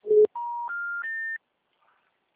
Special information tone.